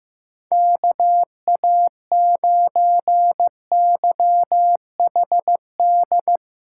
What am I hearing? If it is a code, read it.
Morse code, heard as a keyed tone, about 15 wpm, 700 Hz. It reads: KA9YHD
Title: USA Callsign DE Callsign 15wpm